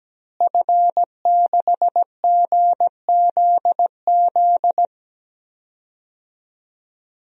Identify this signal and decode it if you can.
Morse code, heard as a keyed tone, about 17 wpm, 700 Hz. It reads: F6GZZ
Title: DXCC-100 Call Signs - 5 Characters - Repeated 3x - 17wpm